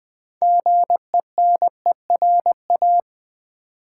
Morse code, heard as a keyed tone, about 20 wpm, 700 Hz. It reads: GENERA
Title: Build A Word - 3 Letters At A Time - 8-Letter Words - 20wpm